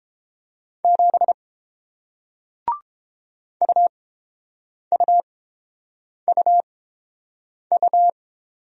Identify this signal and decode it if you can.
Morse code, heard as a keyed tone, about 29 wpm, spaced as for 7 wpm, 700 Hz. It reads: 7 UUUU